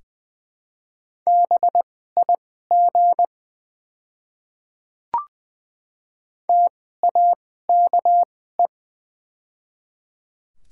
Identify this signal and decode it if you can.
Morse code, heard as a keyed tone, about 20 wpm, spaced as for 14 wpm, 700 Hz. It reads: BIG TAKE
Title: 2x Character Spacing - Top 300 Words - 20wpm